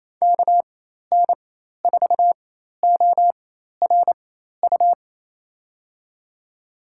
Morse code, heard as a keyed tone, about 28 wpm, spaced as for 13 wpm, 700 Hz. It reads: KN4ORU
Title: USA Call Signs - 4x Character Spacing - 28wpm